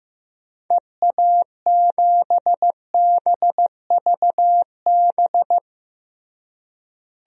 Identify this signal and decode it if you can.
Morse code, heard as a keyed tone, about 15 wpm, 700 Hz. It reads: EA7BVB
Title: DXCC-100 Call Signs - 6 Characters - Repeated 2x - 15wpm